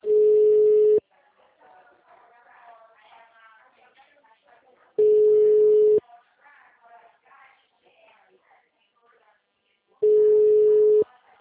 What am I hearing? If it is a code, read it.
Ringback tone (Europe, 425 Hz), with speech.
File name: out- -2012- - - .54786